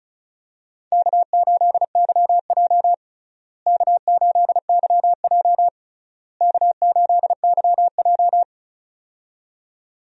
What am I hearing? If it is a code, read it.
Morse code, heard as a keyed tone, about 36 wpm, 700 Hz. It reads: K8YJ K8YJ K8YJ
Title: USA Call Signs - 4 Characters - Repeated 3x - 35wpm